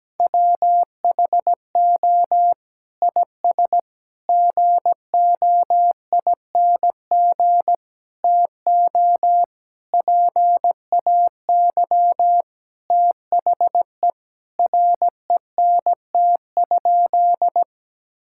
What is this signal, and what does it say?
Morse code, heard as a keyed tone, about 17 wpm, 700 Hz. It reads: WHO IS GOING TO PAY THE RENT?
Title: Sentences from Top 1000 Words - Review Part 5 - 17wpm